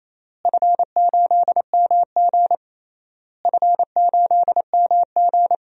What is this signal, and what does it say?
Morse code, heard as a keyed tone, about 28 wpm, 700 Hz. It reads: F8MG F8MG
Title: DXCC-100 Call Signs - 4 Characters - Repeated 3x - 28wpm